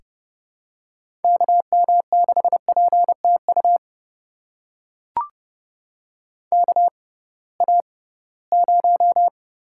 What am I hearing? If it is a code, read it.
Morse code, heard as a keyed tone, about 31 wpm, spaced as for 20 wpm, 700 Hz. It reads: KM6PTU KA0
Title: USA Call Signs - 6 Characters - 6x Character Spacing - 30wpm